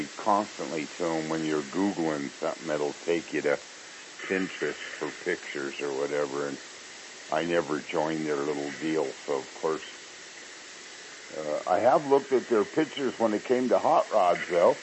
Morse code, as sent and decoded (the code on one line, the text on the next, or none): none